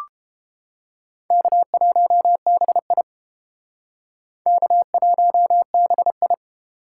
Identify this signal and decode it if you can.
Morse code, heard as a keyed tone, about 31 wpm, 700 Hz. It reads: K1BI K1BI